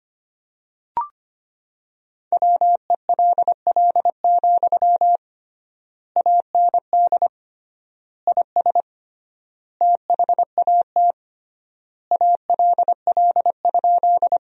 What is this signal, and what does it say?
Morse code, heard as a keyed tone, about 25 wpm, 700 Hz. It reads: WELL, AND IS THAT ALL?
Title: 3x Word-Spacing - Sentences from Top 200 Words - 25wpm